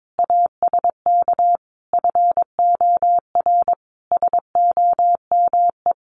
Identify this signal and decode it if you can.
Morse code, heard as a keyed tone, about 22 wpm, 700 Hz. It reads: ASK FOR SOME
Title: Sentences from Top 1500 Words 22wpm